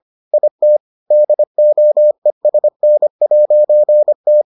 Morse code, heard as a keyed tone, about 25 wpm, 600 Hz. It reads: IT DOESN'T